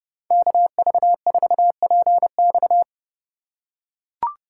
Morse code, heard as a keyed tone, about 30 wpm, 700 Hz. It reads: KV4PX